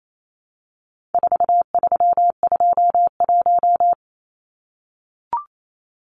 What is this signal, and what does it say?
Morse code, heard as a keyed tone, about 28 wpm, 700 Hz. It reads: 4321
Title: Numbers - 4-Digit 28wpm